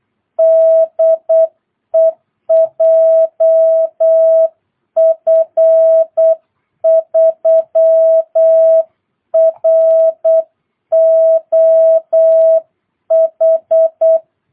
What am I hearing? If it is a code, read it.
Morse code, heard as a keyed tone, about 8 wpm, 650 Hz. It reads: DE JF3ROH
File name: jh8cbh / 2022cw